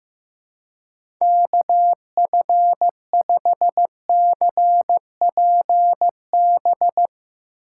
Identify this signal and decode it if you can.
Morse code, heard as a keyed tone, about 15 wpm, 700 Hz. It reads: KF5CPB
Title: USA Call Signs - 6 Characters - 3x Character Spacing - 15wpm